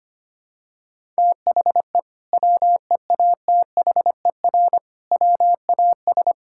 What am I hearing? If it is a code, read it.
Morse code, heard as a keyed tone, about 25 wpm, 700 Hz. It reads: THE WEATHER WAS
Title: Sentences from Top 1000 Words - Review Part 5 - 25wpm